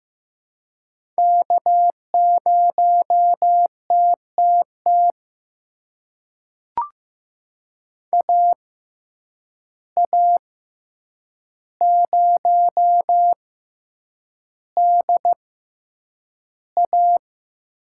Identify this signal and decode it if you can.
Morse code, heard as a keyed tone, about 15 wpm, spaced as for 8 wpm, 700 Hz. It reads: K0TTT AA0DA
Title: USA Call Signs - 5 Characters - 6x Character Spacing - 15wpm